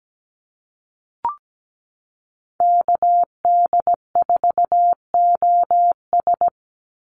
Morse code, heard as a keyed tone, about 17 wpm, 700 Hz. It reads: KD4OS
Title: Call Signs Repeated 3x - 17wpm